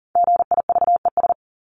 Morse code, heard as a keyed tone, about 40 wpm, 700 Hz. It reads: GIVES
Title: ES:S Suffix - Top 100 Words with Base Words 40wpm